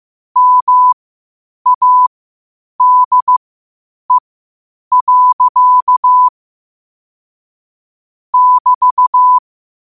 Morse code, heard as a keyed tone, about 15 wpm, spaced as for 8 wpm, 1000 Hz. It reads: MADE. <BT>
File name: qotdfiles / 8wpm